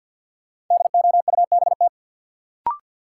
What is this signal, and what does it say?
Morse code, heard as a keyed tone, about 47 wpm, 700 Hz. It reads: DOUBT
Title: Sentences from Top 1500 Words - New Word Drill - 50wpm